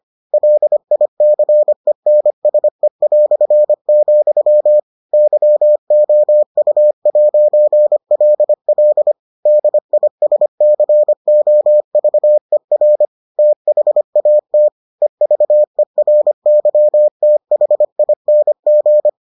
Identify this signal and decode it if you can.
Morse code, heard as a keyed tone, about 25 wpm, 600 Hz. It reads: LICENSE", YOU'LL DISCOVER THAT EVERYTHING